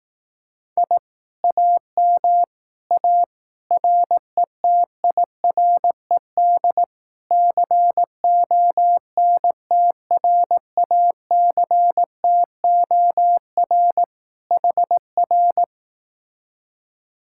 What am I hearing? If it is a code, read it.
Morse code, heard as a keyed tone, about 18 wpm, 700 Hz. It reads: I AM A RETIRED CONTRACTOR HR
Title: Speed-Racing Common QSO Phrases Encore - 15wpm